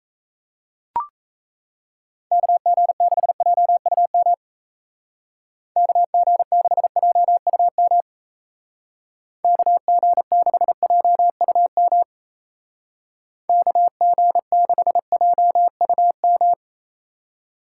Morse code, heard as a keyed tone, about 34 wpm, 700 Hz. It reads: KG6JUM KG6JUM KG6JUM KG6JUM
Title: Speed Racing - USA 2x3 Call Signs 28wpm